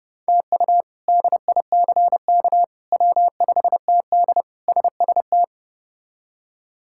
Morse code, heard as a keyed tone, about 30 wpm, 700 Hz. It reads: TU DICK W5TD SST
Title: K1USN SST QSO Practice - 30wpm